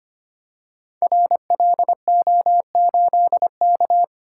Morse code, heard as a keyed tone, about 25 wpm, 700 Hz. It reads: RLO8K
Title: Code Groups - 5 Letter:Numbers - 25wpm